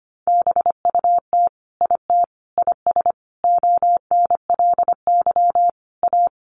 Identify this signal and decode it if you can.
Morse code, heard as a keyed tone, about 25 wpm, 700 Hz. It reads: BUT IT IS ONLY A
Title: Sentences from Top 1500 Words 25wpm